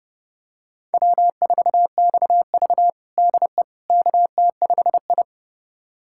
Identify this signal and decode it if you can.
Morse code, heard as a keyed tone, about 30 wpm, 700 Hz. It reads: W4XV DE KT5I
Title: USA Callsign DE Callsign - 4 Characters - 30wpm